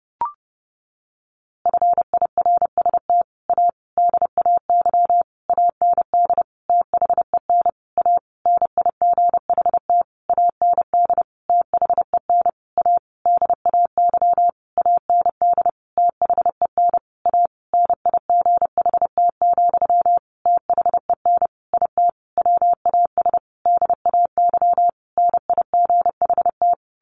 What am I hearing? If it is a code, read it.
Morse code, heard as a keyed tone, about 28 wpm, 700 Hz. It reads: FIRST A DAY AND THEN A NIGHT AND THEN A DAY AND THEN A NIGHT, THEN IT WAS DAY NIGHT
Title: A Sound Of Thunder 30wpm